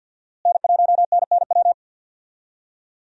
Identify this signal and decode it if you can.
Morse code, heard as a keyed tone, about 47 wpm, 700 Hz. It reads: N1NNW
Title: USA Call Signs - 5 Characters - Repeated 3x - 50wpm